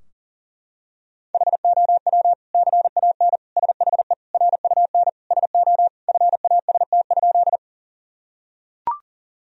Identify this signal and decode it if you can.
Morse code, heard as a keyed tone, about 40 wpm, 700 Hz. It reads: HOW CAN SHE RUN SO FAST?
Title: Sentences from Top 1000 Words - Review Part 2 - 40wpm